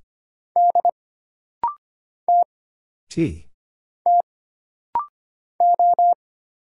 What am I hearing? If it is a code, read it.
Morse code, heard as a keyed tone, about 25 wpm, 700 Hz. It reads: D T T O
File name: Single Character (Letter:Number:Punctuation:Prosign) - Rapid-Fire - 25wpm